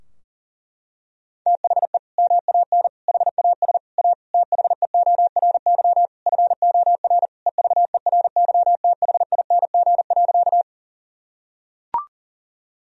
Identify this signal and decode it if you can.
Morse code, heard as a keyed tone, about 40 wpm, 700 Hz. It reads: THE MAN HAS A THEORY FOR EVERYTHING.
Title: Sentences from Top 1000 Words - Review Part 3 - 40wpm